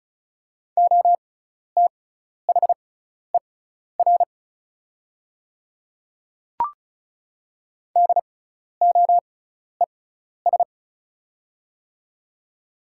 Morse code, heard as a keyed tone, about 34 wpm, spaced as for 12 wpm, 690 Hz. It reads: OTHER DOES
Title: 6x Character Spacing - Top 500 Words - 35wpm